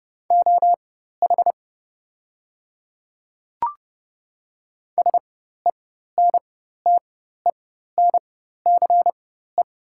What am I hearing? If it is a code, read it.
Morse code, heard as a keyed tone, about 30 wpm, spaced as for 14 wpm, 700 Hz. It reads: OH SENTENCE
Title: 4x Character Spacing - Top 500 Words - 30wpm